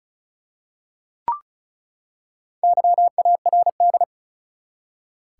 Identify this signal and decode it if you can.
Morse code, heard as a keyed tone, about 34 wpm, 700 Hz. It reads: YARD
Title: Four Letter Words - 35wpm